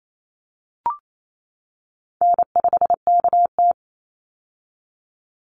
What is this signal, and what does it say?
Morse code, heard as a keyed tone, about 28 wpm, 700 Hz. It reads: N5KT
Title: Easy Call Signs - 28wpm